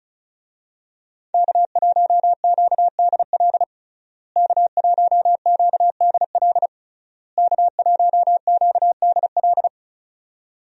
Morse code, heard as a keyed tone, about 34 wpm, 700 Hz. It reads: K1QDL K1QDL K1QDL
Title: USA Call Signs - 5 Characters - Repeated 3x - 35wpm